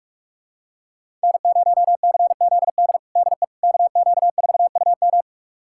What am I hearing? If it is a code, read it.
Morse code, heard as a keyed tone, about 46 wpm, 700 Hz. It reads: N0CZD DE KQ4UM
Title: USA Callsign DE Callsign - 5 Characters - 45wpm